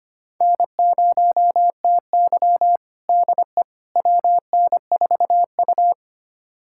Morse code, heard as a keyed tone, about 25 wpm, 700 Hz. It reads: N0TY DE WN4U
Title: USA Callsign DE Callsign - 4 Characters - 25wpm